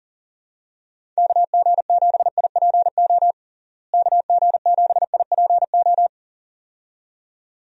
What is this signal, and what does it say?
Morse code, heard as a keyed tone, about 40 wpm, 700 Hz. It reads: KG7IPO KG7IPO
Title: USA Call Signs - 6 Characters - Repeated 2x - 40wpm